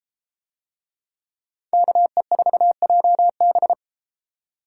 Morse code, heard as a keyed tone, about 33 wpm, 700 Hz. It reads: KE4JB